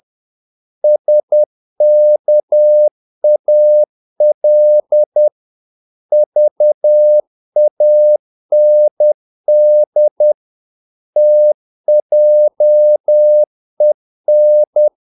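Morse code, heard as a keyed tone, about 10 wpm, 600 Hz. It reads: SKAL VAND TJEN
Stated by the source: Morse practice of short Danish words at 10 wpm, from CW med Gnister.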